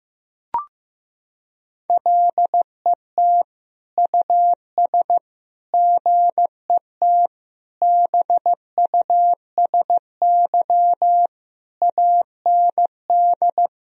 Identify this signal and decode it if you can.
Morse code, heard as a keyed tone, about 15 wpm, 700 Hz. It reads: LET US GET BUSY AND
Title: Sentences from Top 1000 Words - Review Part 3 - 15wpm